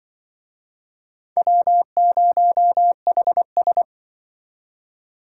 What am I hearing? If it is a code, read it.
Morse code, heard as a keyed tone, about 24 wpm, 700 Hz. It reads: W0HS
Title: Speed Racing - USA 1x2 Call Signs 20wpm